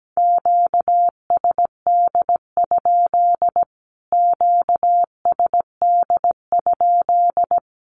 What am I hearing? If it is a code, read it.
Morse code, heard as a keyed tone, about 17 wpm, 700 Hz. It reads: QSD? QSD?
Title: Q-Codes Repeated 3x 17wpm